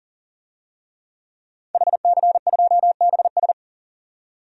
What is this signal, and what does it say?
Morse code, heard as a keyed tone, about 40 wpm, 700 Hz. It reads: HC2BS
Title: DXCC-100 Call Signs - 5 Characters - 40wpm